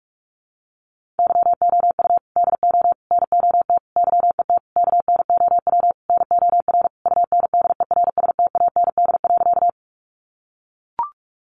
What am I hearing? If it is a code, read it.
Morse code, heard as a keyed tone, about 44 wpm, 700 Hz. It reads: YOU DO NOT YET KNOW NOR UNDERSTAND.
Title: Sentences from Top 1500 Words 45wpm